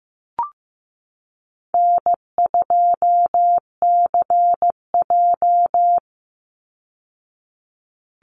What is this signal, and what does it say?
Morse code, heard as a keyed tone, about 15 wpm, 700 Hz. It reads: N2CJ
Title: Easy Call Signs Repeated 3x - 15wpm